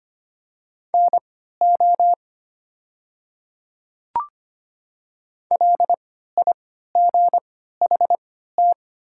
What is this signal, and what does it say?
Morse code, heard as a keyed tone, about 25 wpm, spaced as for 14 wpm, 700 Hz. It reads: NO LIGHT
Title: 3x Character Spacing - Top 200 Words - 25wpm